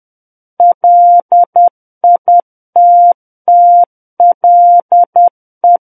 Morse code, heard as a keyed tone, about 10 wpm, 700 Hz. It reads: LITTLE
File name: Book One / 01-Chapter 1 - 10 WPM